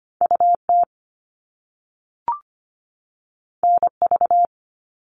Morse code, heard as a keyed tone, about 25 wpm, 700 Hz. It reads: UT NV
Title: US States 25wpm